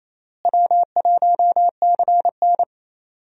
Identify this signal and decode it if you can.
Morse code, heard as a keyed tone, about 28 wpm, 700 Hz. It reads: W1CN
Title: USA Call Signs - 4 Characters - Repeated 3x - 28wpm